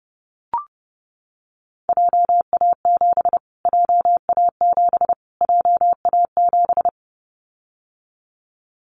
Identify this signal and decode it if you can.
Morse code, heard as a keyed tone, about 30 wpm, 700 Hz. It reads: JA7 JA7 JA7
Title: Callsign Prefixes for Top 25 DXCC - Repeated 3x - 30wpm